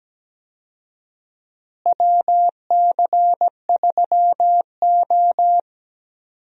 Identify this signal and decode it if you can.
Morse code, heard as a keyed tone, about 17 wpm, 700 Hz. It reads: WC3O
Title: USA Call Signs - 4 Characters - Repeated 2x - 17wpm